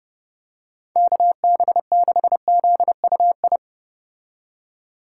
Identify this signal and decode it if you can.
Morse code, heard as a keyed tone, about 29 wpm, 700 Hz. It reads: KB6ZUI